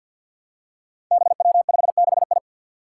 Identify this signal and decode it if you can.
Morse code, heard as a keyed tone, about 47 wpm, 700 Hz. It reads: BW56I
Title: Five Challenging Letter-Number Combinations 50wpm